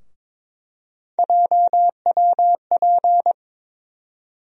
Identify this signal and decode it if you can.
Morse code, heard as a keyed tone, about 21 wpm, 700 Hz. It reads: JWP
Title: USA Call Sign Postfixes - 3 Letters 22wpm